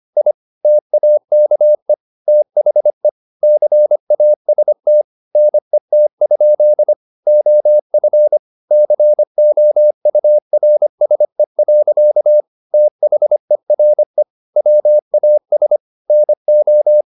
Morse code, heard as a keyed tone, about 25 wpm, 600 Hz. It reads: I TAKE THE CAST NET? OF COURSE. THERE WAS NO
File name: the old man and the sea part 1 25wpm